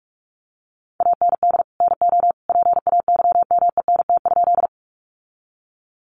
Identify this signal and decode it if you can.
Morse code, heard as a keyed tone, about 44 wpm, 700 Hz. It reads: AND NO PAYMENT?
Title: Sentences from Top 1200 Words 45wpm